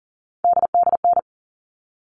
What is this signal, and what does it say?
Morse code, heard as a keyed tone, about 40 wpm, 700 Hz. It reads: DDN
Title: All Cut Numbers - 3-Digit 40wpm